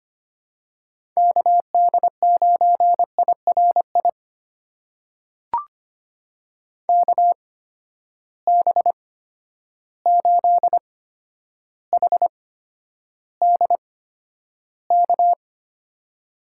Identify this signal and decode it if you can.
Morse code, heard as a keyed tone, about 25 wpm, spaced as for 11 wpm, 700 Hz. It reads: KD9IRI KB8HDK
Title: USA Call Signs - 6 Characters - 8x Character Spacing - 25wpm